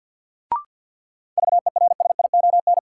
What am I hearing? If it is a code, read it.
Morse code, heard as a keyed tone, about 48 wpm, 700 Hz. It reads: VERSION
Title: Sentences from Top 1400 Words - New Word Drill - 50wpm